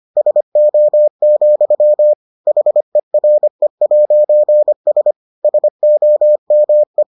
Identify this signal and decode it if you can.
Morse code, heard as a keyed tone, about 25 wpm, 600 Hz. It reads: SO, HERE'S SOME